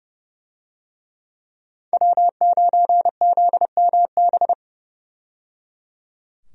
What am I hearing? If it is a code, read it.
Morse code, heard as a keyed tone, about 30 wpm, 700 Hz. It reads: W9ZMB
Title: USA Call Signs - 5 Characters - 30wpm